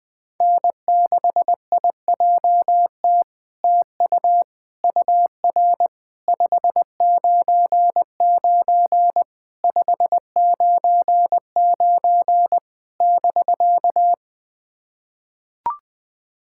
Morse code, heard as a keyed tone, about 21 wpm, 700 Hz. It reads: N6IJT TU UR 599 599 <BK>
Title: SOTA QSO Practice 20wpm